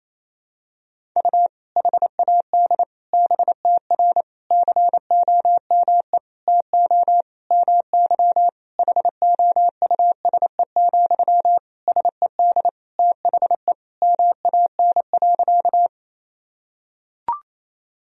Morse code, heard as a keyed tone, about 28 wpm, 700 Hz. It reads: U HAD BTR COME TO MY HOUSE, SED THE MAN.
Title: Sentences from Top 400 Words and Abbreviations 28wpm